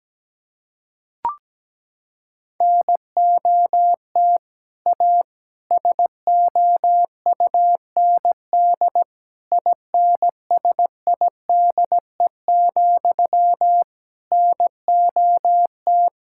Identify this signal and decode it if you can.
Morse code, heard as a keyed tone, about 17 wpm, 700 Hz. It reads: NOT A SOUND INSIDE, NOT
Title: Sentences from Top 1000 Words - Review Part 6 - 17wpm